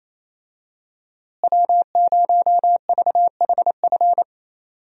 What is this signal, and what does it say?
Morse code, heard as a keyed tone, about 28 wpm, 700 Hz. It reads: W0VHF
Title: USA Call Signs - 5 Characters - Repeated 3x - 28wpm